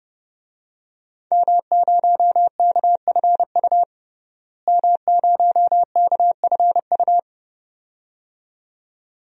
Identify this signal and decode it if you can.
Morse code, heard as a keyed tone, about 31 wpm, 700 Hz. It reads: M0KFU M0KFU
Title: DXCC-100 Call Signs - 5 Characters - Repeated 2x - 30wpm